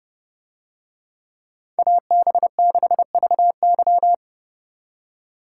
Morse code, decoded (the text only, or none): AB6VY